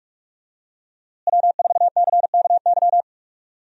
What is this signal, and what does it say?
Morse code, heard as a keyed tone, about 45 wpm, 700 Hz. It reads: W4CKY
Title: USA Call Signs - 5 Characters - Repeated 3x - 45wpm